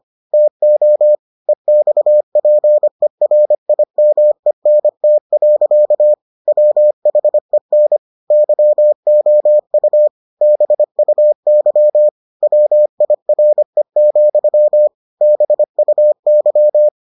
Morse code, heard as a keyed tone, about 25 wpm, 600 Hz. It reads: TO EXPERIMENT. WHEN YOU BUY WIRE, BUY